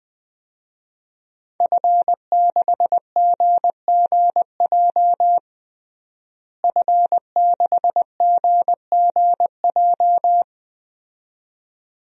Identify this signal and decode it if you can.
Morse code, heard as a keyed tone, about 20 wpm, 700 Hz. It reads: F6GGJ F6GGJ
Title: DXCC-100 Call Signs - 5 Characters - Repeated 2x - 20wpm